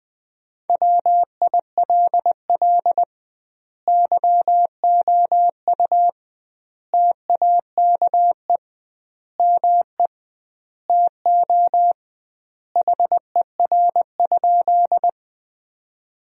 Morse code, heard as a keyed tone, about 20 wpm, 700 Hz. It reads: WILL YOU TAKE ME TO HER?
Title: 2x Word-Spacing - Sentences from Top 200 Words - 20wpm